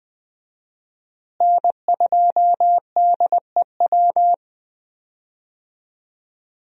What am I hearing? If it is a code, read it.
Morse code, heard as a keyed tone, about 20 wpm, 700 Hz. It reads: N2DEW
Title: USA Call Signs - 5 Characters - Repeated 3x - 20wpm